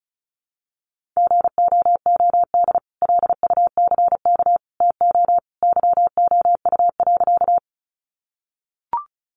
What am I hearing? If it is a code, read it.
Morse code, heard as a keyed tone, about 34 wpm, 700 Hz. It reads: GOOD LUCK TO YOU.
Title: Sentences from Top 1500 Words 35wpm